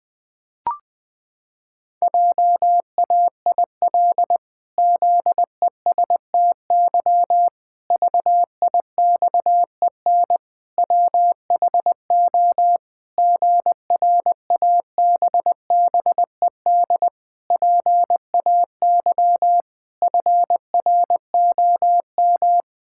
Morse code, heard as a keyed tone, about 21 wpm, 700 Hz. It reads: JAIL ZESTY VIXEN WHO GRABBED PAY FROM
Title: Pangram Challenge v2 - 20wpm